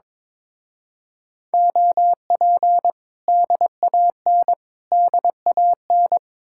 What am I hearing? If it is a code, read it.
Morse code, heard as a keyed tone, about 24 wpm, 700 Hz. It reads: OP DAN DAN
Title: Speed-Racing Common QSO Phrases Encore - 22wpm